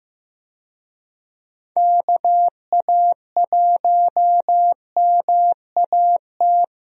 Morse code, heard as a keyed tone, about 15 wpm, 700 Hz. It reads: KA1MAT
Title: USA Call Signs - 6 Characters - Repeated 3x - 15wpm